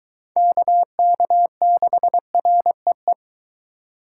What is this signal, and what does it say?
Morse code, heard as a keyed tone, about 23 wpm, 700 Hz. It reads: KK6REE